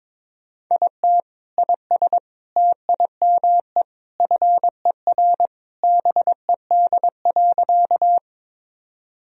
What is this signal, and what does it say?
Morse code, heard as a keyed tone, about 22 wpm, 700 Hz. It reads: IT IS TIME FER BED.
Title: Sentences from Top 500 Words and Abbreviations 22wpm